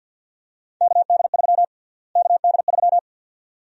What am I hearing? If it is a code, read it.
Morse code, heard as a keyed tone, about 52 wpm, 700 Hz. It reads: KD3 KD3
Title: USA Call Sign Prefixes - Repeated 2x - 50wpm